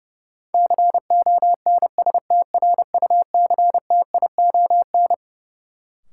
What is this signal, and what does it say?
Morse code, heard as a keyed tone, about 30 wpm, 700 Hz. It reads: CONSTRUCTION
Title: Sentences from Top 1600 Words - New Word Drill - 30wpm